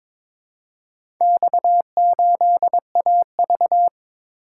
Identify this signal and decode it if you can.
Morse code, heard as a keyed tone, about 22 wpm, 700 Hz. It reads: X8AV
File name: practice / 22wpm / Four Letter-Number Combinations - 22wpm